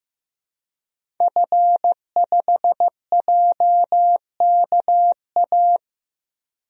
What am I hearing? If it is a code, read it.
Morse code, heard as a keyed tone, about 15 wpm, 700 Hz. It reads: F5JKA
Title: DXCC-100 Call Signs - 5 Characters - 15wpm